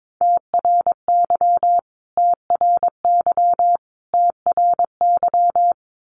Morse code, heard as a keyed tone, about 22 wpm, 700 Hz. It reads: TRY TRY TRY
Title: Top 500 Words Repeated 22wpm